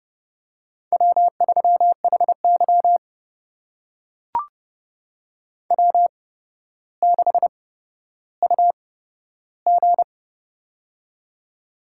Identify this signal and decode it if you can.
Morse code, heard as a keyed tone, about 30 wpm, spaced as for 13 wpm, 700 Hz. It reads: W3HY W6UG
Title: USA Call Signs - 4 Characters - 8x Character Spacing - 30wpm